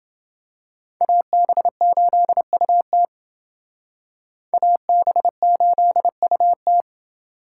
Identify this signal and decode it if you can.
Morse code, heard as a keyed tone, about 28 wpm, 700 Hz. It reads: AB8UT AB8UT